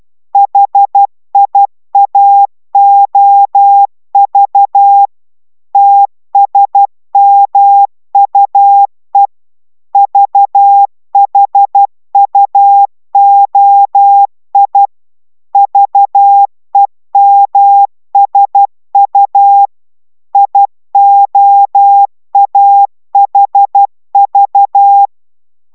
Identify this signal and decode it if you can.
Morse code, heard as a keyed tone, about 12 wpm, 800 Hz. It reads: HIAOV TSMUE VHUOI VEMSU IOAHV
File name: taviro tanulas / cw 69 05 10